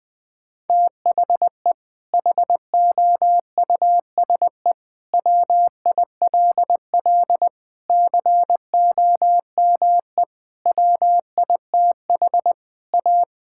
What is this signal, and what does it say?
Morse code, heard as a keyed tone, about 20 wpm, 700 Hz. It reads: THE HOUSE WILL COME WITH A
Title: Sentences from Top 1600 Words 20wpm